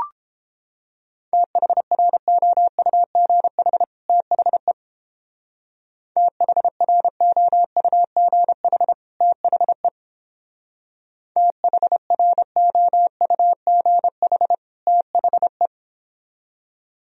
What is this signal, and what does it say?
Morse code, heard as a keyed tone, about 29 wpm, 700 Hz. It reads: THROUGH THE THROUGH THE THROUGH THE